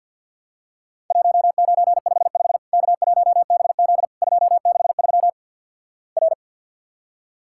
Morse code, heard as a keyed tone, about 54 wpm, 700 Hz. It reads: 1955 K1BZ 263 R
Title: Worked All Europe DX Contest - QTC Exchange - 50wpm